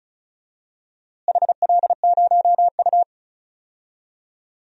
Morse code, heard as a keyed tone, about 36 wpm, 700 Hz. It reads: HL0U
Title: DXCC-100 Call Signs - 4 Characters - Repeated 2x - 35wpm